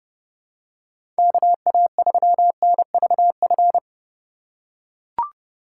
Morse code, heard as a keyed tone, about 30 wpm, 700 Hz. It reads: KA3NVF
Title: USA Call Signs - 6 Characters - 4x Character Spacing - 30wpm